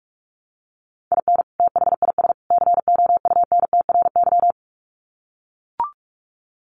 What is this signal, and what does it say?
Morse code, heard as a keyed tone, about 44 wpm, 700 Hz. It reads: IN THIS COUNTRY
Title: Sets of 3 Words 45wpm